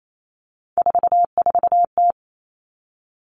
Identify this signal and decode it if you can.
Morse code, heard as a keyed tone, about 28 wpm, 700 Hz. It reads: 44T
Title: Three Challenging Letter-Number Combinations 28wpm